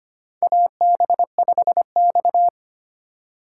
Morse code, heard as a keyed tone, about 25 wpm, 700 Hz. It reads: AB5X
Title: USA Call Signs - 4 Characters - 4x Character Spacing - 25wpm